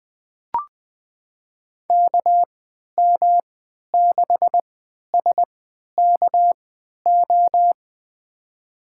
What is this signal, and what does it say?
Morse code, heard as a keyed tone, about 21 wpm, spaced as for 11 wpm, 700 Hz. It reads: KM6SKO
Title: USA Call Signs - 6 Characters - 3x Character Spacing - 20wpm